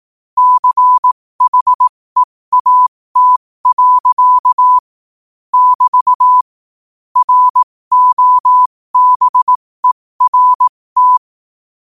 Morse code, heard as a keyed tone, about 18 wpm, spaced as for 15 wpm, 1000 Hz. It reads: CHEAT. <BT> ROBERT
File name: qotdfiles / 15wpm